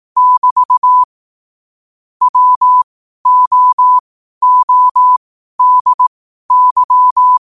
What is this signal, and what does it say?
Morse code, heard as a keyed tone, about 18 wpm, spaced as for 12 wpm, 1000 Hz. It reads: <BT> WOODY
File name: qotdfiles / 12wpm